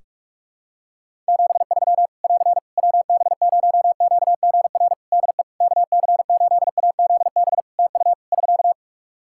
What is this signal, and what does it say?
Morse code, heard as a keyed tone, about 46 wpm, 700 Hz. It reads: 73 <AR> WB0QGR DE KC8AZB TU <SK>
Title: Practice QSO - Exchange 4 - Ending - 45wpm